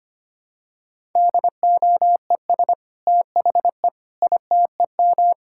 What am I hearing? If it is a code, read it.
Morse code, heard as a keyed tone, about 25 wpm, 700 Hz. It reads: DOES THE ITEM
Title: Sentences from Top 1000 Words - Review Part 6 - 25wpm